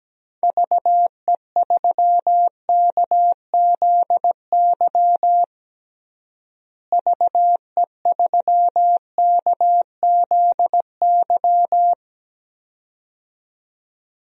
Morse code, heard as a keyed tone, about 17 wpm, 700 Hz. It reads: VE3KZY VE3KZY
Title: DXCC-100 Call Signs - 6 Characters - Repeated 2x - 17wpm